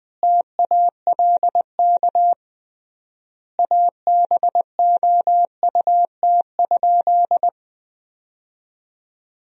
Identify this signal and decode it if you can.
Morse code, heard as a keyed tone, about 21 wpm, 700 Hz. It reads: TALK ABOUT?
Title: 3x Word-Spacing - Sentences from Top 400 Words - 20wpm